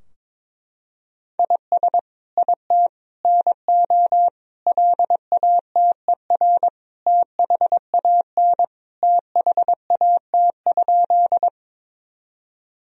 Morse code, heard as a keyed tone, about 22 wpm, 700 Hz. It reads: IS IT NO LATER THAN THAT?
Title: Sentences from Top 1000 Words - Review Part 4 - 22wpm